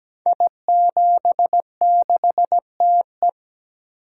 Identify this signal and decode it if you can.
Morse code, heard as a keyed tone, about 17 wpm, 700 Hz. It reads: I76TE
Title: Code Groups - 5 Letter:Numbers - 17wpm